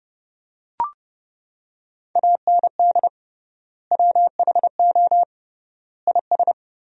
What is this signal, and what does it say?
Morse code, heard as a keyed tone, about 30 wpm, 700 Hz. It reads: AND WHO IS
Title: 3x Word-Spacing - Sentences from Top 300 Words - 30wpm